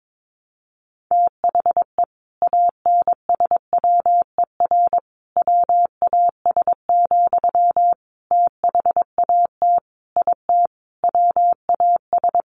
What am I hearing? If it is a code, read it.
Morse code, heard as a keyed tone, about 22 wpm, 700 Hz. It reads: THE ANSWER WAS, THAT IT WAS
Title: Sentences from Top 1100 Words 22wpm